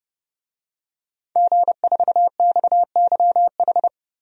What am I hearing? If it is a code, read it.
Morse code, heard as a keyed tone, about 30 wpm, 700 Hz. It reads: G4XYH
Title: DXCC-100 Call Signs - 5 Characters - 30wpm